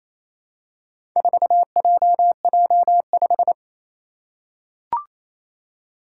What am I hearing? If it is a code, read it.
Morse code, heard as a keyed tone, about 28 wpm, 700 Hz. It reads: 4JJ5
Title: Four Challenging Letter-Number Combinations 28wpm